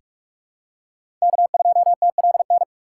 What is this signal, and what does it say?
Morse code, heard as a keyed tone, about 46 wpm, 700 Hz. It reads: K2TLN
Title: USA Call Signs - 5 Characters - 2x Character Spacing - 45wpm